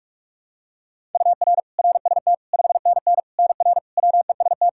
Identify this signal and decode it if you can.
Morse code, heard as a keyed tone, about 46 wpm, 700 Hz. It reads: UR RST 5NN NR WEST
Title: Practice QSO - Exchange 1 - Beginning - 45wpm